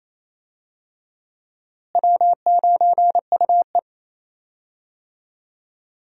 Morse code, heard as a keyed tone, about 27 wpm, 700 Hz. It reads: W9UE
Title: USA Call Signs - 4 Characters - Repeated 3x - 28wpm